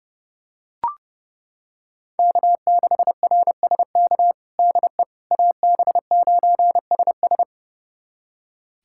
Morse code, heard as a keyed tone, about 30 wpm, 700 Hz. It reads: K6RSK DE AB9SS
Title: USA Callsign DE Callsign - 5 Characters - 30wpm